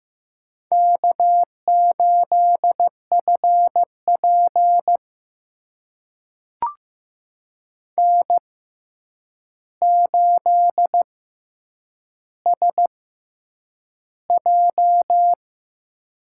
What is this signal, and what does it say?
Morse code, heard as a keyed tone, about 15 wpm, spaced as for 8 wpm, 700 Hz. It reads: K8FP N8SJ
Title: USA Call Signs - 4 Characters - 6x Character Spacing - 15wpm